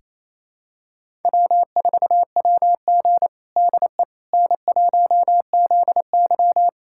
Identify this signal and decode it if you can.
Morse code, heard as a keyed tone, about 28 wpm, 700 Hz. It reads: W4WG DE N1ZY
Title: USA Callsign DE Callsign - 4 Characters - 28wpm